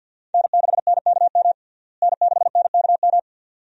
Morse code, heard as a keyed tone, about 46 wpm, 700 Hz. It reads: N6NKM N6NKM
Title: USA Call Signs - 5 Characters - Repeated 2x - 50wpm